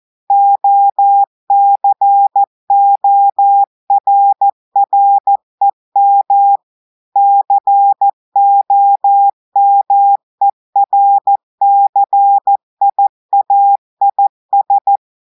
Morse code, heard as a keyed tone, about 14 wpm, 800 Hz. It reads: OCORREM COMERCIAIS
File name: audios / portugais / portugais texte en clair 2 vitesse 14 mots minutes